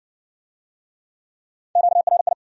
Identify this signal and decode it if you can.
Morse code, heard as a keyed tone, about 66 wpm, 700 Hz. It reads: QRI